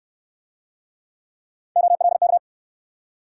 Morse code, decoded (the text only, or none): OLK